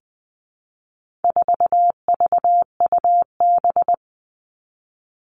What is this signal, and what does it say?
Morse code, heard as a keyed tone, about 20 wpm, 700 Hz. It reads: 4VUB
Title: All Cut Numbers - 4-Digit 20wpm